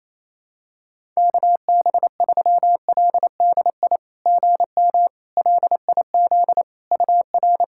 Morse code, heard as a keyed tone, about 28 wpm, 700 Hz. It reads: KB3LDI GM LIZ UR
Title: SOTA QSO Practice 28wpm